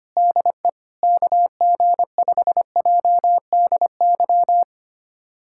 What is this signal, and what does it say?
Morse code, heard as a keyed tone, about 25 wpm, 700 Hz. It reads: DE KG5JDY
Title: USA Callsign DE Callsign 25wpm